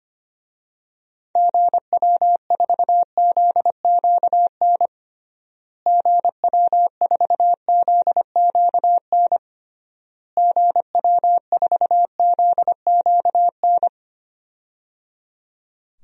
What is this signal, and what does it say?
Morse code, heard as a keyed tone, about 25 wpm, 700 Hz. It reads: GW4ZQN GW4ZQN GW4ZQN
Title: DXCC-100 Call Signs - 6 Characters - Repeated 3x - 25wpm